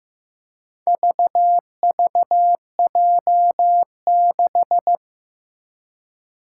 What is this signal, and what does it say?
Morse code, heard as a keyed tone, about 15 wpm, 700 Hz. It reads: VVJ6
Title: Four Challenging Letter-Number Combinations 15wpm